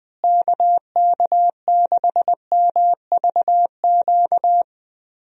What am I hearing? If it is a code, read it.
Morse code, heard as a keyed tone, about 21 wpm, 700 Hz. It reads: KK6MVQ
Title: USA Call Signs - 6 Characters - Repeated 3x - 20wpm